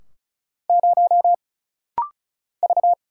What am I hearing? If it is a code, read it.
Morse code, heard as a keyed tone, about 36 wpm, 700 Hz. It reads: 0 V
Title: Single Letter-Number - Rapid-Fire 35wpm